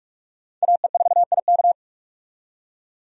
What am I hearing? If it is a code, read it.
Morse code, heard as a keyed tone, about 46 wpm, 700 Hz. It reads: AE4IK
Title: USA Call Signs - 5 Characters - 45wpm